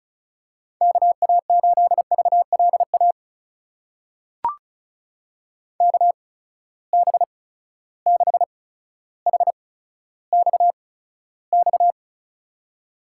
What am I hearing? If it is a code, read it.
Morse code, heard as a keyed tone, about 36 wpm, spaced as for 15 wpm, 700 Hz. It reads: KA8VLA KB6HXX
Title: USA Call Signs - 6 Characters - 8x Character Spacing - 35wpm